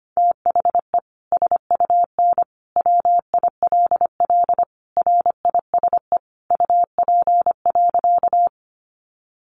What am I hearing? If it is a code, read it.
Morse code, heard as a keyed tone, about 25 wpm, 700 Hz. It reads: THE SUN WILL RISE UP.
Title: Sentences from Top 900 Words 25wpm